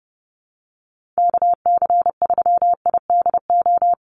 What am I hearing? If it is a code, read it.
Morse code, heard as a keyed tone, about 30 wpm, 700 Hz. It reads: KC3IDO
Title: USA Call Signs 30wpm